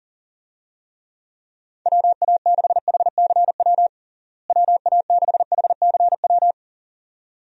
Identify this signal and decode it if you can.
Morse code, heard as a keyed tone, about 40 wpm, 700 Hz. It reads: WA6HCW WA6HCW
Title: USA Call Signs - 6 Characters - Repeated 2x - 40wpm